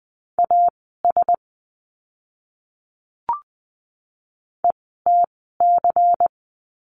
Morse code, heard as a keyed tone, about 20 wpm, spaced as for 14 wpm, 700 Hz. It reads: AS ETC
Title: 094 Lesson 16f - Letter c - Words 20wpm